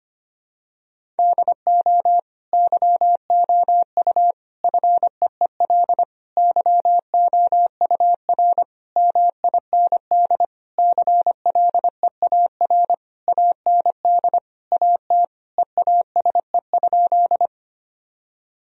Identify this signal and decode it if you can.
Morse code, heard as a keyed tone, about 25 wpm, 700 Hz. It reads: DO YOU FEEL YOUR MIND CLEAR AND AT EASE?
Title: Sentences from Top 1000 Words - Review Part 1 - 25wpm